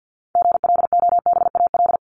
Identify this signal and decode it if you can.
Morse code, heard as a keyed tone, about 50 wpm, 700 Hz. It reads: GLOBAL
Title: Top 1001-1250 Words 50wpm